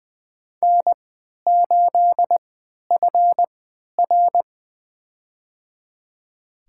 Morse code, heard as a keyed tone, about 20 wpm, spaced as for 11 wpm, 700 Hz. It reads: N8FR
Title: USA Call Signs - 4 Characters - 3x Character Spacing - 20wpm